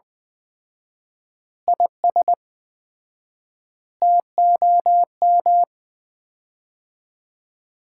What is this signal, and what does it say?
Morse code, heard as a keyed tone, about 21 wpm, 700 Hz. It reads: IS TOM
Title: Common QSO Phrases - 4x Word-Spacing - 20wpm